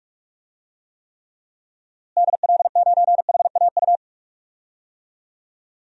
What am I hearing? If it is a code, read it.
Morse code, heard as a keyed tone, about 41 wpm, 700 Hz. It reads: DL9HAU
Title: DXCC-100 Call Signs - 6 Characters - Repeated 3x - 45wpm